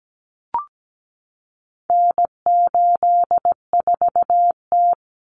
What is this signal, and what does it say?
Morse code, heard as a keyed tone, about 17 wpm, 700 Hz. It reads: N84T
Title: Common Cut Numbers - 4-Digit 17wpm